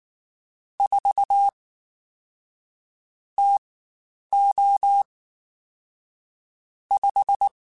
Morse code, heard as a keyed tone, about 19 wpm, spaced as for 9 wpm, 800 Hz. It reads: 4 TO 5
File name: Spaced Characters / Spaced Characters-mp3s / Weather 4 21 sp @18